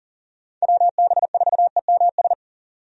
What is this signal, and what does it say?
Morse code, heard as a keyed tone, about 40 wpm, 700 Hz. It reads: WB4EMS
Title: USA Call Signs - 6 Characters - 40wpm